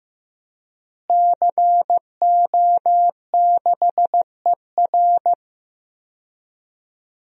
Morse code, heard as a keyed tone, about 16 wpm, 700 Hz. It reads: CO6ER
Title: DXCC-100 Call Signs - 5 Characters - Repeated 3x - 15wpm